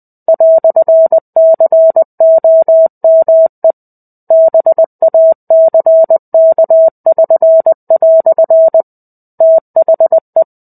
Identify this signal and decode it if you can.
Morse code, heard as a keyed tone, about 20 wpm, 650 Hz. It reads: "COME BACK<SN>" THE
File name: 05-Chapter-05 20 WPM